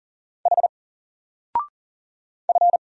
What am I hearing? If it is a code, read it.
Morse code, heard as a keyed tone, about 39 wpm, 700 Hz. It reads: H F
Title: Single Character (Letter:Number:Punctuation:Prosign) 40wpm